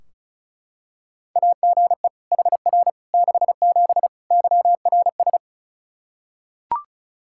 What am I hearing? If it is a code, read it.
Morse code, heard as a keyed tone, about 34 wpm, 700 Hz. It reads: AGE HR 67 YRS
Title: K1USN SST QSO Practice 35wpm